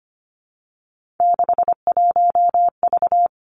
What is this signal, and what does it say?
Morse code, heard as a keyed tone, about 25 wpm, 700 Hz. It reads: 61V
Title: Three Challenging Letter-Number Combinations 25wpm